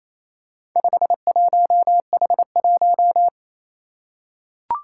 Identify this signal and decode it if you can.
Morse code, heard as a keyed tone, about 28 wpm, 700 Hz. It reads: 51H1
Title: Four Challenging Letter-Number Combinations 28wpm